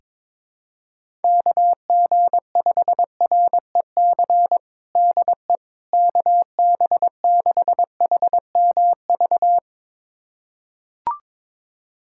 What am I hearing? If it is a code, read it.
Morse code, heard as a keyed tone, about 23 wpm, 700 Hz. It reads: KG5REC DE KB6HMV
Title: USA Callsign DE Callsign - 6 Characters - 22wpm